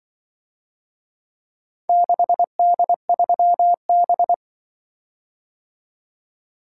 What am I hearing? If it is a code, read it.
Morse code, heard as a keyed tone, about 24 wpm, 700 Hz. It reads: 6D3B
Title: Speed-Racing - Four Letter-Number Combinations - 20wpm